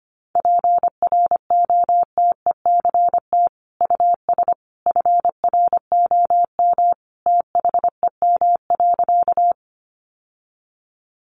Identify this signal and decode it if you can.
Morse code, heard as a keyed tone, about 25 wpm, 700 Hz. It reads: PROTECT US FROM THEM.
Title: Sentences from Top 900 Words 25wpm